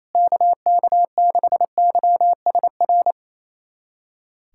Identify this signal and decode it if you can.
Morse code, heard as a keyed tone, about 28 wpm, 700 Hz. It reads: KK6YSR
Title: USA Callsign DE Callsign - 6 Characters - 28wpm